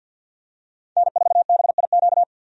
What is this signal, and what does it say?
Morse code, heard as a keyed tone, about 48 wpm, 700 Hz. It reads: N4BIQ
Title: USA Call Signs - 5 Characters - 50wpm